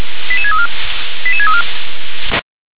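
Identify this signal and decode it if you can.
Five-tone selective call (ZVEI), heard as quick runs of five tones, tones of about 70 ms. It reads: 88634 88634
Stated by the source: ZVEI five-tone selective call received by SDR.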